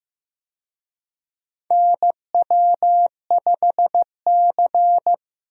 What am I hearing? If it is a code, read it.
Morse code, heard as a keyed tone, about 16 wpm, 700 Hz. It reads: NW5C
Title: USA Call Signs - 4 Characters - Repeated 3x - 15wpm